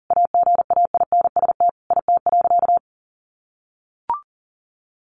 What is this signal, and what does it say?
Morse code, heard as a keyed tone, about 40 wpm, 700 Hz. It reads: AGAINST IT.
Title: Sentences from Top 1500 Words 40wpm